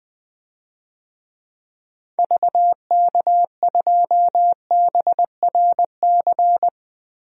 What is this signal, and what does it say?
Morse code, heard as a keyed tone, about 20 wpm, 700 Hz. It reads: VK2BRC
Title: DXCC-100 Call Signs - 6 Characters - Repeated 3x - 20wpm